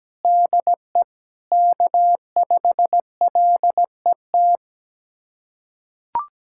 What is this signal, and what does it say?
Morse code, heard as a keyed tone, about 17 wpm, 700 Hz. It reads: DE K5LET
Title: USA Callsign DE Callsign - 5 Characters - 17wpm